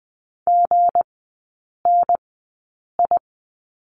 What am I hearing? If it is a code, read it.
Morse code, heard as a keyed tone, about 20 wpm, spaced as for 8 wpm, 690 Hz. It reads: GNI
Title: 172 Lesson 28e - Number 9 - Three Characters 20wpm